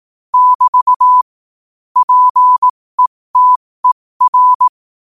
Morse code, heard as a keyed tone, about 18 wpm, spaced as for 15 wpm, 1000 Hz. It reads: <BT> PETER